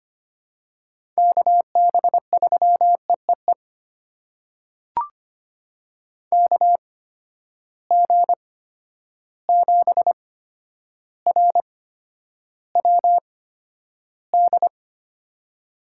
Morse code, heard as a keyed tone, about 25 wpm, spaced as for 11 wpm, 700 Hz. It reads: KB3EEE KG7RWD
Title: USA Call Signs - 6 Characters - 8x Character Spacing - 25wpm